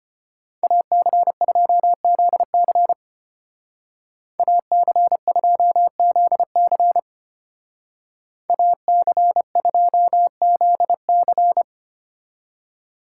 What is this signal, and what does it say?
Morse code, heard as a keyed tone, about 33 wpm, 700 Hz. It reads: AC2ZC AC2ZC AC2ZC